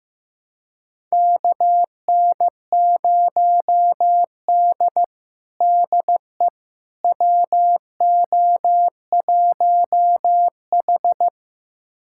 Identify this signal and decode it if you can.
Morse code, heard as a keyed tone, about 15 wpm, 700 Hz. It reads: KN0D DE WO1H
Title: USA Callsign DE Callsign - 4 Characters - 15wpm